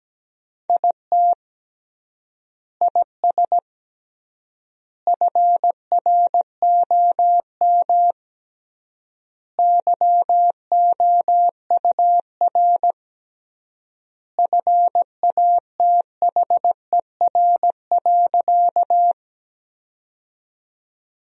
Morse code, heard as a keyed tone, about 17 wpm, 700 Hz. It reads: IT IS FROM YOUR FATHER.
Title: 3x Word-Spacing - Sentences from Top 300 Words - 17wpm